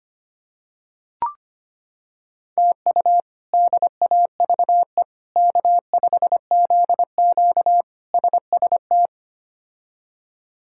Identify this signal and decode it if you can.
Morse code, heard as a keyed tone, about 25 wpm, 700 Hz. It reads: TU DAVE K5ZQ SST
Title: K1USN SST QSO Practice - 25wpm